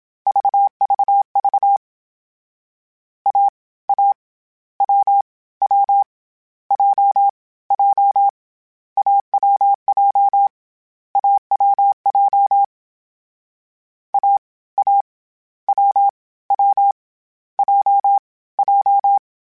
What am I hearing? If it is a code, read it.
Morse code, heard as a keyed tone, about 27 wpm, 800 Hz. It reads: VVV A A W W J J AWJ AWJ A A W W J J
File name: Difficult Combos / Difficult Combos-mp3s / awj1 ndb6 gz7 @25